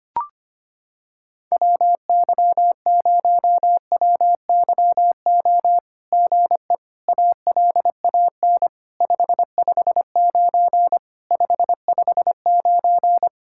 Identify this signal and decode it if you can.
Morse code, heard as a keyed tone, about 25 wpm, 700 Hz. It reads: WY0WYO GE ALAN 559 559
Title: SOTA QSO Practice 25wpm